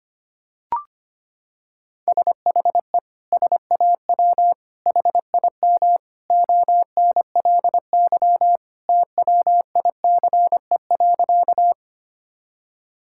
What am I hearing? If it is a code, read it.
Morse code, heard as a keyed tone, about 25 wpm, 700 Hz. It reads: SHE SAW HIM ONLY TWICE.
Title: Sentences from Top 1600 Words 25wpm